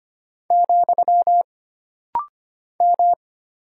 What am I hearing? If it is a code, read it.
Morse code, heard as a keyed tone, about 25 wpm, 700 Hz. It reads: , M
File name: Single Character (Letter:Number:Punctuation:Prosign) - Rapid-Fire - 25wpm